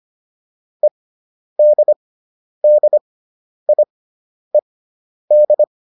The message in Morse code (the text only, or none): EDDIED